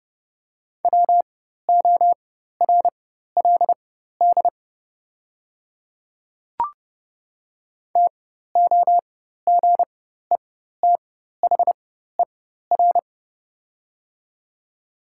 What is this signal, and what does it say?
Morse code, heard as a keyed tone, about 30 wpm, spaced as for 14 wpm, 700 Hz. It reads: WORLD TOGETHER
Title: 4x Character Spacing - Top 500 Words - 30wpm